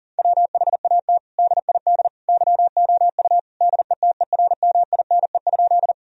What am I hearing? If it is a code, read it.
Morse code, heard as a keyed tone, about 40 wpm, 700 Hz. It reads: WHAT DID YOU DETERMINE?
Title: Sentences from Top 1000 Words - Review Part 4 - 40wpm